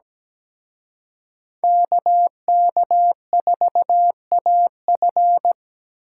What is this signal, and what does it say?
Morse code, heard as a keyed tone, about 20 wpm, 700 Hz. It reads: KK4AF
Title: Speed Racing - USA 2x2 Call Signs 17wpm